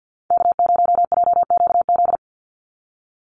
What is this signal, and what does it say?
Morse code, heard as a keyed tone, about 48 wpm, 700 Hz. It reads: K0JQZ
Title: NA SOTA Chaser Callsigns 50wpm